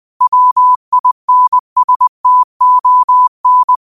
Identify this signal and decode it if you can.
Morse code, heard as a keyed tone, about 20 wpm, 1000 Hz. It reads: WINSTON